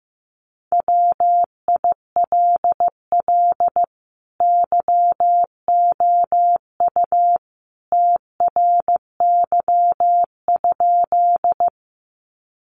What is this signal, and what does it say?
Morse code, heard as a keyed tone, about 15 wpm, 700 Hz. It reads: WILL YOU TRY?
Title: Sentences from Top 200 Words 15wpm